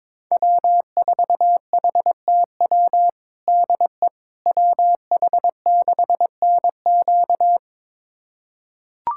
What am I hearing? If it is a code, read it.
Morse code, heard as a keyed tone, about 22 wpm, 700 Hz. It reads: W4HTW DE WH6NQ
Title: USA Callsign DE Callsign - 5 Characters - 22wpm